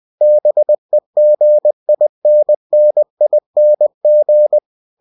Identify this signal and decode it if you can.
Morse code, heard as a keyed tone, about 20 wpm, 600 Hz. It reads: BEGINNING